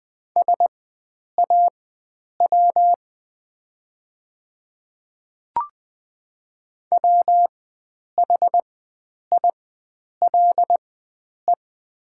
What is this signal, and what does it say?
Morse code, heard as a keyed tone, about 20 wpm, spaced as for 9 wpm, 700 Hz. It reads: SAW WHILE
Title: 4x Character Spacing - Top 300 Words - 20wpm